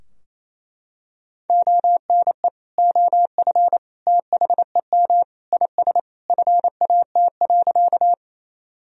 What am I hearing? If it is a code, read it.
Morse code, heard as a keyed tone, about 28 wpm, 700 Hz. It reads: ONE OF THEM IS FAT.
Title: Sentences from Top 1000 Words - Review Part 4 - 28wpm